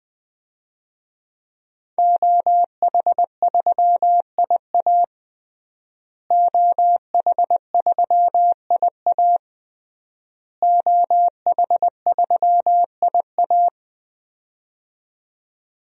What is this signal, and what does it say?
Morse code, heard as a keyed tone, about 21 wpm, 700 Hz. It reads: OH3IA OH3IA OH3IA
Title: DXCC-100 Call Signs - 5 Characters - Repeated 3x - 20wpm